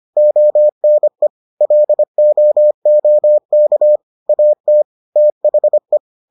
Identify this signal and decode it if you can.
Morse code, heard as a keyed tone, about 25 wpm, 600 Hz. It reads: ONE LOOK AT THE